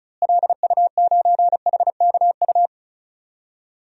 Morse code, heard as a keyed tone, about 34 wpm, 700 Hz. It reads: LU9HKU
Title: DXCC-100 Call Signs - 6 Characters - Repeated 3x - 35wpm